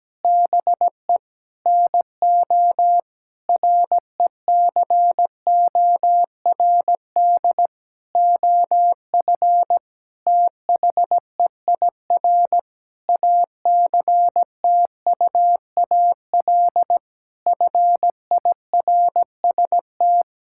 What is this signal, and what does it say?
Morse code, heard as a keyed tone, about 17 wpm, 700 Hz. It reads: BE NO RECORD OF THEIR ACTUAL FIRST
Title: Sentences from Top 1000 Words - Review Part 5 - 17wpm